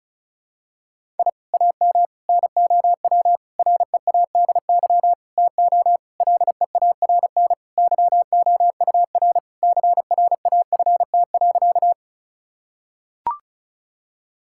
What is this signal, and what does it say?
Morse code, heard as a keyed tone, about 36 wpm, 700 Hz. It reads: I AM NOW READY TO LEARN YOUR CRAFT.
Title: Sentences from Top 1000 Words - Review Part 6 - 35wpm